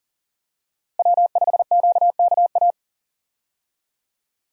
Morse code, heard as a keyed tone, about 40 wpm, 700 Hz. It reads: W5QKA
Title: USA Call Signs - 5 Characters - Repeated 2x - 40wpm